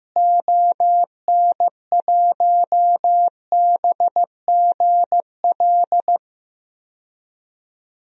Morse code, heard as a keyed tone, about 15 wpm, 700 Hz. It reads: ON1BGL
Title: DXCC-100 Call Signs - 6 Characters - Repeated 2x - 15wpm